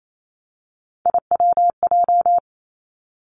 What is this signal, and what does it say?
Morse code, heard as a keyed tone, about 28 wpm, 700 Hz. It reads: IWJ
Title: Three Challenging Letter-Number Combinations 28wpm